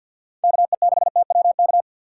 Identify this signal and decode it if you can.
Morse code, heard as a keyed tone, about 47 wpm, 700 Hz. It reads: KE6TWK
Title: USA Call Signs - 6 Characters - 6x Character Spacing - 50wpm